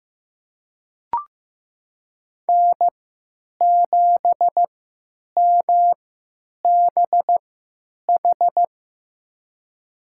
Morse code, heard as a keyed tone, about 15 wpm, spaced as for 9 wpm, 700 Hz. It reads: N7MBH
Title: USA Call Signs - 3x Character Spacing - 15wpm